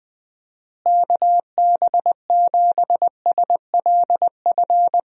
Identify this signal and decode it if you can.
Morse code, heard as a keyed tone, about 20 wpm, 700 Hz. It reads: KB7SLF
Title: USA Call Signs - 6 Characters - Repeated 2x - 20wpm